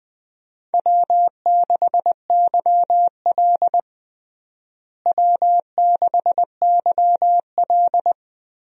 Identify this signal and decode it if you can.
Morse code, heard as a keyed tone, about 20 wpm, 700 Hz. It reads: W6YL W6YL
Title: USA Call Signs - 4 Characters - Repeated 3x - 20wpm